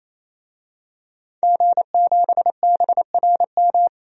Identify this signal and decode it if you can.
Morse code, heard as a keyed tone, about 27 wpm, 700 Hz. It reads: G7BRM
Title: DXCC-100 Call Signs - 5 Characters - 28wpm